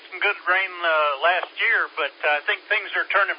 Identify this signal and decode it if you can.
Emergency Alert System (SAME) tones, heard as harsh buzzing bursts, with speech. It is heard from 0.4 to 3.0 s.